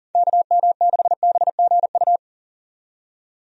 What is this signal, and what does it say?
Morse code, heard as a keyed tone, about 40 wpm, 700 Hz. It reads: KM6BGU
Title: USA Call Signs - 6 Characters - 40wpm